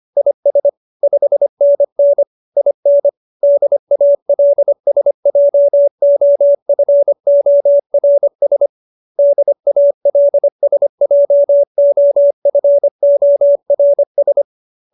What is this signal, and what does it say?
Morse code, heard as a keyed tone, about 25 wpm, 570 Hz. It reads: IS 5NN IN DALSJOFORS DALSJOFORS